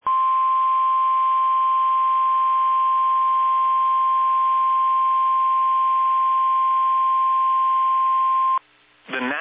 Emergency Alert System (SAME) tones (the 1050 Hz attention tone). It is heard from 0.0 to 8.6 s.
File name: eas / KBBX / KBBX